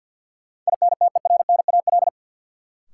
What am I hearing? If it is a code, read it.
Morse code, heard as a keyed tone, about 52 wpm, 700 Hz. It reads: INTERNAL
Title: Sentences from Top 1400 Words - New Word Drill - 50wpm